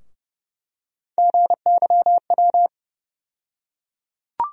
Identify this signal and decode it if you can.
Morse code, heard as a keyed tone, about 30 wpm, 700 Hz. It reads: GYW
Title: USA Call Sign Postfixes - 3 Letters 30wpm